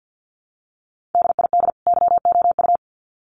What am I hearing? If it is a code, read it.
Morse code, heard as a keyed tone, about 50 wpm, 700 Hz. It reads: DID YOU
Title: Sets of 2 Words 50wpm